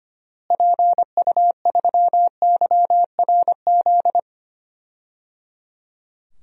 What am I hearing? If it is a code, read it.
Morse code, heard as a keyed tone, about 25 wpm, 700 Hz. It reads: PU3YRZ
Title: DXCC-100 Call Signs - 6 Characters - Repeated 3x - 25wpm